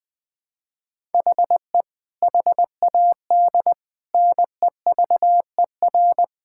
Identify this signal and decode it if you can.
Morse code, heard as a keyed tone, about 21 wpm, 700 Hz. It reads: HE HAD NEVER
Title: Sentences from Top 1000 Words - Review Part 5 - 20wpm